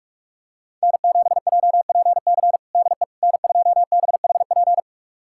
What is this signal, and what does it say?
Morse code, heard as a keyed tone, about 42 wpm, 700 Hz. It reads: N7JPC DE N2BHP
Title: USA Callsign DE Callsign - 5 Characters - 45wpm